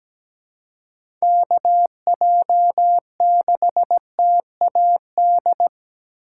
Morse code, heard as a keyed tone, about 17 wpm, 700 Hz. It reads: KJ6TAD
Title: USA Call Signs - 3x Character Spacing - 17wpm